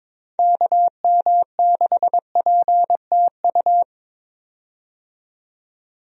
Morse code, heard as a keyed tone, about 22 wpm, 700 Hz. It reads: KM6PTU
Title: USA Call Signs - 6 Characters - 22wpm